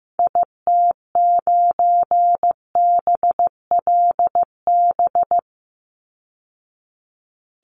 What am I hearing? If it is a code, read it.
Morse code, heard as a keyed tone, about 15 wpm, 700 Hz. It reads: IT9BLB
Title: Call Signs Repeated 3x - 15wpm